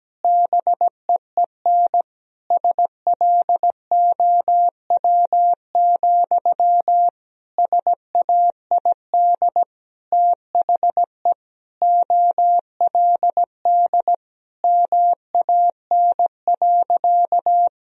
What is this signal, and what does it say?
Morse code, heard as a keyed tone, about 17 wpm, 700 Hz. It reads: BEEN SLOW, SAID THE OLD MAN.
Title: Sentences from Top 1000 Words - Review Part 2 - 17wpm